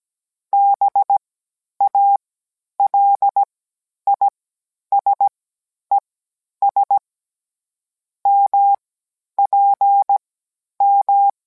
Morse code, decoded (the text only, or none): BALISES MPM